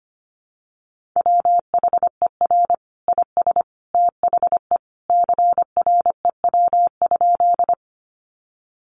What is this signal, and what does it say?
Morse code, heard as a keyed tone, about 25 wpm, 700 Hz. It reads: WHER IS THE CREW?
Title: Sentences from Top 1500 Words 25wpm